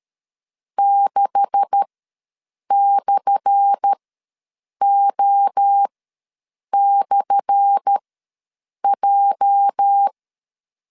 Morse code, heard as a keyed tone, about 13 wpm, spaced as for 7 wpm, 800 Hz. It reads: 6/O/J
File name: LEZIONI / 10 SIGNS / 10A SLASH ws 12wpm 3443